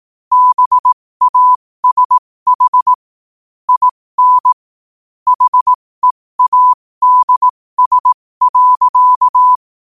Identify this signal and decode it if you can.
Morse code, heard as a keyed tone, about 18 wpm, spaced as for 15 wpm, 1000 Hz. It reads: BASH IN HEADS.
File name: qotdfiles / 15wpm